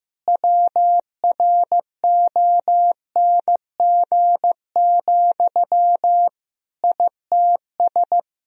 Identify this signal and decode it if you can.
Morse code, heard as a keyed tone, about 15 wpm, 700 Hz. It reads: WRONG, ITS
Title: Sentences from Top 1000 Words - Review Part 3 - 15wpm